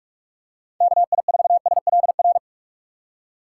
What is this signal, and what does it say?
Morse code, heard as a keyed tone, about 46 wpm, 700 Hz. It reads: KI4SLR
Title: USA Call Signs - 6 Characters - 6x Character Spacing - 45wpm